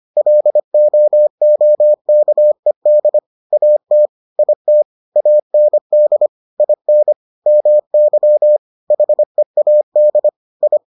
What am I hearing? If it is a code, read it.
Morse code, heard as a keyed tone, about 25 wpm, 600 Hz. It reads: LOOKED AT IT AND IN MY HEAD I